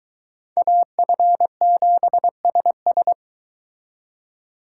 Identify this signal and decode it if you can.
Morse code, heard as a keyed tone, about 22 wpm, 700 Hz. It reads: AF7SS